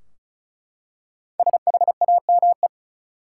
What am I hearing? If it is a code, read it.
Morse code, heard as a keyed tone, about 36 wpm, 700 Hz. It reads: SHAME
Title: Five Letter Words - 35wpm